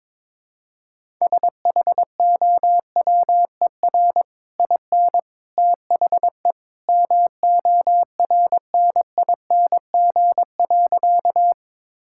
Morse code, heard as a keyed tone, about 22 wpm, 700 Hz. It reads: SHOWER IN THE MORNING.
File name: practice / 22wpm / Easy Sentences with Hints - 22wpm